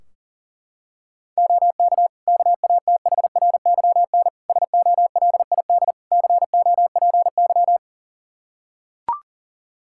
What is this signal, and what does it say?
Morse code, heard as a keyed tone, about 40 wpm, 700 Hz. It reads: OK KATHRYN SOLID COPY
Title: K1USN SST QSO Practice 40wpm